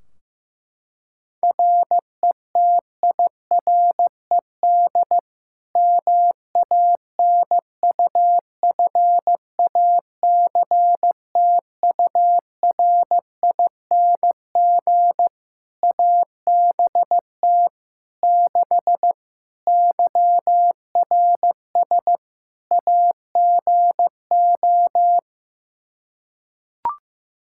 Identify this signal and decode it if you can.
Morse code, heard as a keyed tone, about 15 wpm, 700 Hz. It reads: RETIRED MANUFACTURING ABT 6 YRS AGO
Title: Practice QSO - Exchange 3 - More Info - 15wpm